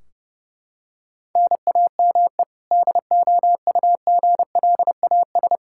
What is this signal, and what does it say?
Morse code, heard as a keyed tone, about 30 wpm, 700 Hz. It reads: NAME DOUGLAS
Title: Practice QSO - Exchange 1 - Beginning - 30wpm